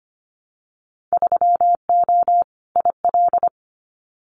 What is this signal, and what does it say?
Morse code, heard as a keyed tone, about 25 wpm, 700 Hz. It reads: 3O IL
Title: Winter Field Day Exchanges 25wpm